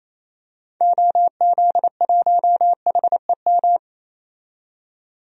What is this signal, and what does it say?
Morse code, heard as a keyed tone, about 28 wpm, 700 Hz. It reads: OZ1HEM
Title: DXCC-100 Call Signs - 6 Characters - 28wpm